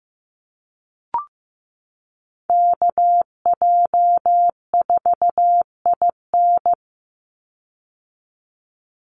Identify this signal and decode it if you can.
Morse code, heard as a keyed tone, about 15 wpm, 700 Hz. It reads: KJ4IN
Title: USA Call Signs 15wpm